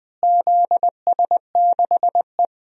Morse code, heard as a keyed tone, about 20 wpm, 700 Hz. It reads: ZS6E
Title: DXCC-100 Call Signs - 4 Characters - Repeated 2x - 20wpm